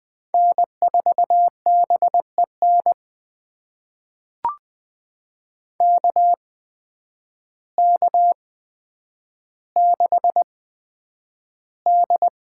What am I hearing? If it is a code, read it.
Morse code, heard as a keyed tone, about 20 wpm, spaced as for 9 wpm, 700 Hz. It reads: N4BEN KK6D
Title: USA Call Signs - 5 Characters - 8x Character Spacing - 20wpm